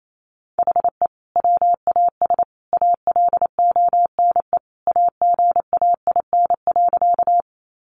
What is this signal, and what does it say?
Morse code, heard as a keyed tone, about 28 wpm, 700 Hz. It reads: HE WAS ALONE AGAIN.
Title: Sentences from Top 1000 Words 28wpm